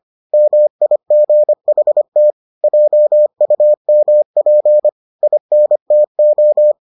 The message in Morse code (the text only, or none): MIGHT JUMP INTO